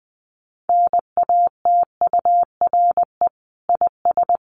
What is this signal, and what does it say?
Morse code, heard as a keyed tone, about 20 wpm, 700 Hz. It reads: NATURE IS
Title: Sentences from Top 1500 Words 20wpm